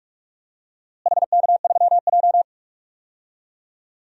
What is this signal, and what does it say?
Morse code, heard as a keyed tone, about 42 wpm, 700 Hz. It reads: HK3J
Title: DXCC-100 Call Signs - 4 Characters - Repeated 3x - 45wpm